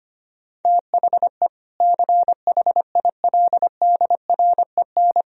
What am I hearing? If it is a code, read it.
Morse code, heard as a keyed tone, about 25 wpm, 700 Hz. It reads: THE CHILDREN
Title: Sentences from Top 1000 Words - Review Part 1 - 25wpm